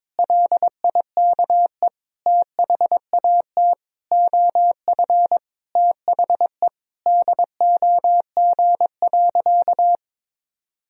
Morse code, heard as a keyed tone, about 22 wpm, 700 Hz. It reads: LIKE THAT OF THE DOG.
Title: Sentences from Top 1000 Words - Review Part 2 - 22wpm